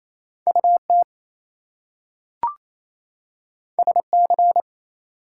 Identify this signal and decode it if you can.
Morse code, heard as a keyed tone, about 28 wpm, 700 Hz. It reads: UT SC
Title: US States and Canadian Provinces 28wpm